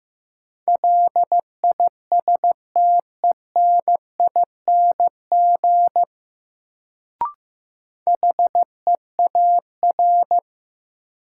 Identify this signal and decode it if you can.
Morse code, heard as a keyed tone, about 16 wpm, 700 Hz. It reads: LISTENING HEAR
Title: Sets of 5 Words - New Word Drill - 15wpm